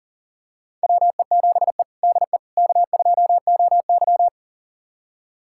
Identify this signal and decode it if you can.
Morse code, heard as a keyed tone, about 37 wpm, 700 Hz. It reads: WE7E DE K2OY
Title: USA Callsign DE Callsign - 4 Characters - 40wpm